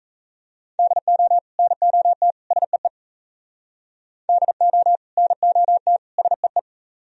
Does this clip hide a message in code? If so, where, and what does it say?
Yes: Morse code from 0.8 to 6.6 s, heard as a keyed tone, about 37 wpm, 700 Hz. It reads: DO NOT SEE DO NOT SEE